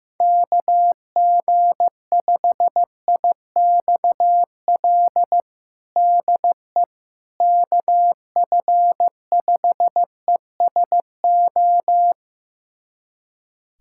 Morse code, heard as a keyed tone, about 16 wpm, 700 Hz. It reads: KG5IXL DE KF5ESO
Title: USA Callsign DE Callsign - 6 Characters - 15wpm